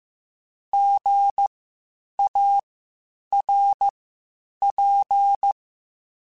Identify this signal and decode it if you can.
Morse code, heard as a keyed tone, about 15 wpm, spaced as for 8 wpm, 800 Hz. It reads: GARP
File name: Spaced Alphas / Spaced Alphas mp3s / Gs sp @13